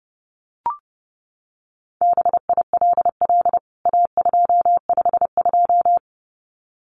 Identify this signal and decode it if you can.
Morse code, heard as a keyed tone, about 30 wpm, 700 Hz. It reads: BILL A252
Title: CWT Exchange 30wpm